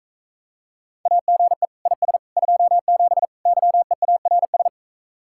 Morse code, heard as a keyed tone, about 45 wpm, 700 Hz. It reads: AGE IS 27 YEARS